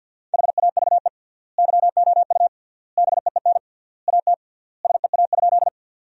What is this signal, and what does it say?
Morse code, heard as a keyed tone, about 46 wpm, 700 Hz. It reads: HAVE YOU BEEN AT SEA?
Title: 3x Word-Spacing - Sentences from Top 300 Words - 50wpm